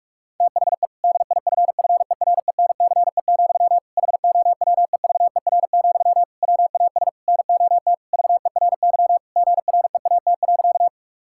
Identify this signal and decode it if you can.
Morse code, heard as a keyed tone, about 41 wpm, 700 Hz. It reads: THE DIFFERENCE, HOWEVER, WAS NOT VERY GREAT.
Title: Sentences from Top 1000 Words - Review Part 5 - 45wpm